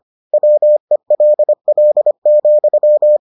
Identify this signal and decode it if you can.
Morse code, heard as a keyed tone, about 25 wpm, 600 Hz. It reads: WELL,